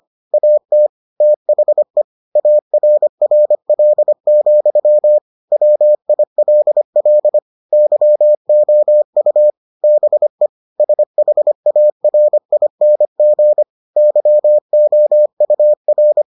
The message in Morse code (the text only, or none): AT THE ARRL, WILL YOU BE SHARING YOUR